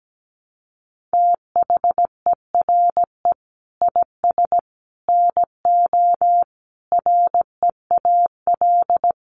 Morse code, heard as a keyed tone, about 17 wpm, 700 Hz. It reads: THERE IS NO REAL
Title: Sentences from Top 1000 Words 17wpm